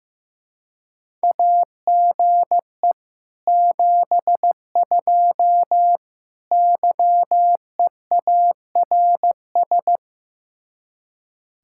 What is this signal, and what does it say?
Morse code, heard as a keyed tone, about 15 wpm, 700 Hz. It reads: AGE 72 YEARS
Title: Speed-Racing Common QSO Phrases Encore - 15wpm